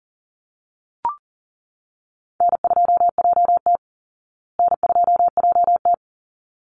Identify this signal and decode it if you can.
Morse code, heard as a keyed tone, about 39 wpm, 700 Hz. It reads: N2JT N2JT
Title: Easy Call Signs Repeated 2x - 40wpm